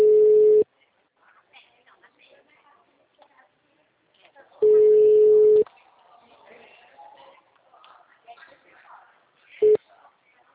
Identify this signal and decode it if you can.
Ringback tone (Europe, 425 Hz), with speech.